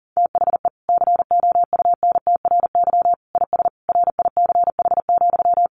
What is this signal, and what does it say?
Morse code, heard as a keyed tone, about 40 wpm, 700 Hz. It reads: THE COUNTRY IS RICH,
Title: Sentences from Top 1100 Words 40wpm